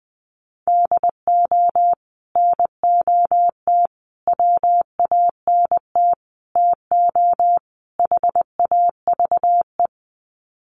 Morse code, heard as a keyed tone, about 20 wpm, 700 Hz. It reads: DO NOT WANT TO HAVE
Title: Sets of 5 Words 20wpm